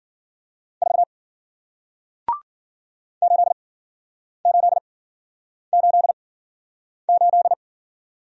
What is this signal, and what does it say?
Morse code, heard as a keyed tone, about 53 wpm, spaced as for 9 wpm, 700 Hz. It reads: 4 8888